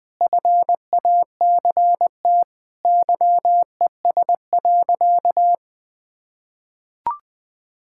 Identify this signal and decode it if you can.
Morse code, heard as a keyed tone, about 20 wpm, 700 Hz. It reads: FACT YES.
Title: Sentences with 4-Word Phrases - Spoken Beginning - 20wpm